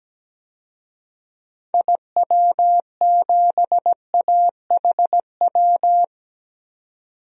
Morse code, heard as a keyed tone, about 17 wpm, 700 Hz. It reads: IW7AHW
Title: DXCC-100 Call Signs - 6 Characters - 17wpm